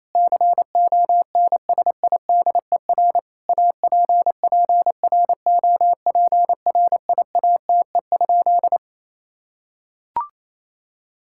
Morse code, heard as a keyed tone, about 28 wpm, 700 Hz. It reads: CONSIDER APPROPRIATE?
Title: Sentences from Top 1000 Words - Review Part 5 - 28wpm